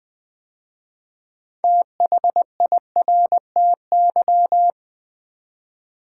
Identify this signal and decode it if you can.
Morse code, heard as a keyed tone, about 20 wpm, 700 Hz. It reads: THIRTY
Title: 6 Letter Words - First 3 Letters Given - 20wpm